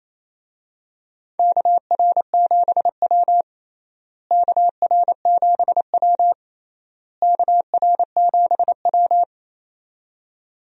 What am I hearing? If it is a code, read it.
Morse code, heard as a keyed tone, about 28 wpm, 700 Hz. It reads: KR7W KR7W KR7W
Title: USA Call Signs - 4 Characters - Repeated 3x - 28wpm